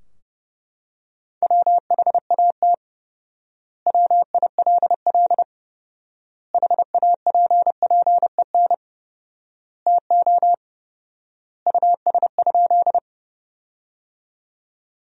Morse code, heard as a keyed tone, about 30 wpm, 700 Hz. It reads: WHAT WILL HAPPEN TO US?
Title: 4x Word-Spacing - Sentences from Top 400 Words - 30wpm